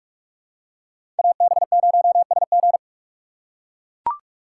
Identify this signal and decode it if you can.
Morse code, heard as a keyed tone, about 42 wpm, 700 Hz. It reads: AB0SG
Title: USA Call Signs - 2x Character Spacing - 45wpm